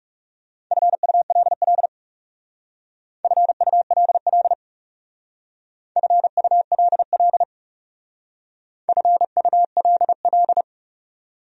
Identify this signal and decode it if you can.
Morse code, heard as a keyed tone, about 36 wpm, 700 Hz. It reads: FULL FULL FULL FULL
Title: Speed-Racing Top 500 Words 30wpm